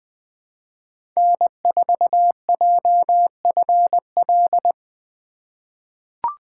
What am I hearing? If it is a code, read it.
Morse code, heard as a keyed tone, about 20 wpm, 700 Hz. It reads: N4JFL
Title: USA Call Signs - 5 Characters - 3x Character Spacing - 20wpm